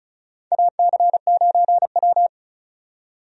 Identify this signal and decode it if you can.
Morse code, heard as a keyed tone, about 36 wpm, 700 Hz. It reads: AC9W
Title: USA Call Signs - 4 Characters - 35wpm